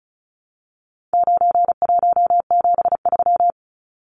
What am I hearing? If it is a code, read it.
Morse code, heard as a keyed tone, about 34 wpm, 700 Hz. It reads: 9173
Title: Numbers - 4-Digit 35wpm